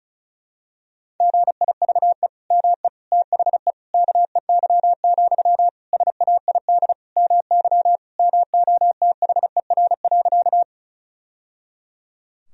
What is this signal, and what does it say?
Morse code, heard as a keyed tone, about 36 wpm, 700 Hz. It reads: GIVE ME THE KEY, SAID MY MOTHER.
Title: Sentences from Top 1000 Words - Review Part 4 - 35wpm